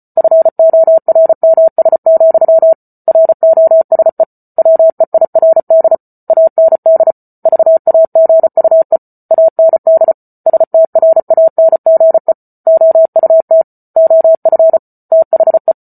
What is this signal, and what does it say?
Morse code, heard as a keyed tone, about 34 wpm, 650 Hz. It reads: FORMS, ROSE WEIRD AND VAGUE AND STRANGE OUT OF THE